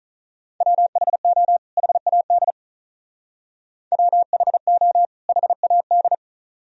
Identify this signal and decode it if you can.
Morse code, heard as a keyed tone, about 47 wpm, 700 Hz. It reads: WHO HAD WHO HAD